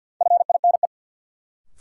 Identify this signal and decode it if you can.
Morse code, heard as a keyed tone, about 52 wpm, 700 Hz. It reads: FINE
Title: Sentences from Top 500 Words - New Word Drill - 50wpm